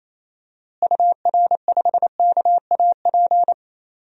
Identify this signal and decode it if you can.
Morse code, heard as a keyed tone, about 27 wpm, 700 Hz. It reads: UR5KAP
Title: DXCC-100 Call Signs - 6 Characters - Repeated 3x - 28wpm